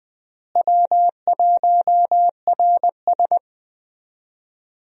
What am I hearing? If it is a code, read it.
Morse code, heard as a keyed tone, about 20 wpm, 700 Hz. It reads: W1RS
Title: USA Call Signs - 4 Characters - Repeated 3x - 20wpm